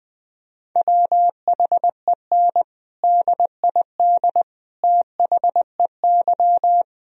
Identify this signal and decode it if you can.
Morse code, heard as a keyed tone, about 20 wpm, 700 Hz. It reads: WHEN DID THEY
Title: Sentences from Top 1000 Words - Review Part 2 - 20wpm